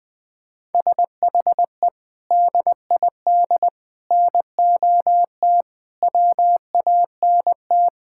Morse code, heard as a keyed tone, about 20 wpm, 700 Hz. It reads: SHE DID NOT WANT
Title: Sentences with 4-Word Phrases - Spoken Beginning - 20wpm